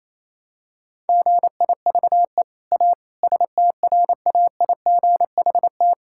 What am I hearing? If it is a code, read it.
Morse code, heard as a keyed tone, about 28 wpm, 700 Hz. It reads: GIVE A STRAIGHT
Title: Sentences from Top 1000 Words - Review Part 6 - 28wpm